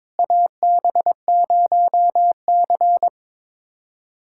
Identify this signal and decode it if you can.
Morse code, heard as a keyed tone, about 22 wpm, 700 Hz. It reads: AB0C
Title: USA Callsign DE Callsign - 4 Characters - 22wpm